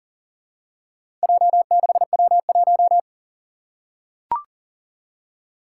Morse code, heard as a keyed tone, about 40 wpm, 700 Hz. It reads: J6W1
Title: Four Challenging Letter-Number Combinations 40wpm